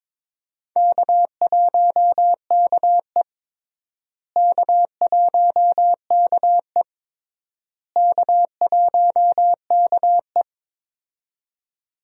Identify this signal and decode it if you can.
Morse code, heard as a keyed tone, about 21 wpm, 700 Hz. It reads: K1KE K1KE K1KE
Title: USA Call Signs - 4 Characters - Repeated 3x - 22wpm